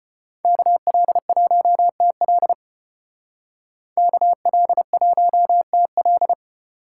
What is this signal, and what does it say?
Morse code, heard as a keyed tone, about 31 wpm, 700 Hz. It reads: KL1TL KL1TL